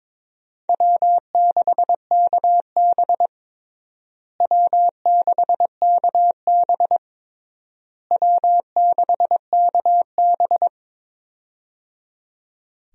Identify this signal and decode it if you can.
Morse code, heard as a keyed tone, about 22 wpm, 700 Hz. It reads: W6KB W6KB W6KB
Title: USA Call Signs - 4 Characters - Repeated 3x - 22wpm